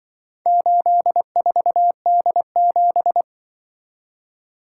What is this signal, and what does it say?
Morse code, heard as a keyed tone, about 25 wpm, 700 Hz. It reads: 84D7
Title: Speed-Racing - Four Letter-Number Combinations - 20wpm